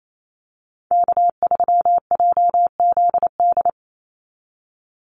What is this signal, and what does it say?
Morse code, heard as a keyed tone, about 28 wpm, 700 Hz. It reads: K3JZD
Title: NA SOTA Chaser Callsigns 28wpm